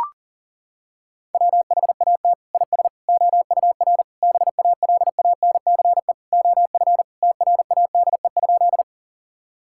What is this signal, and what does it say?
Morse code, heard as a keyed tone, about 40 wpm, 700 Hz. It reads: WHAT IS OUR BALANCE OF TRADE?
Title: Sentences from Top 1000 Words - Review Part 6 - 40wpm